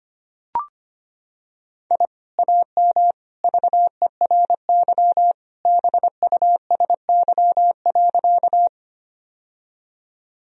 Morse code, heard as a keyed tone, about 25 wpm, 700 Hz. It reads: I AM VERY BUSY.
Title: Sentences from Top 1000 Words - Review Part 3 - 25wpm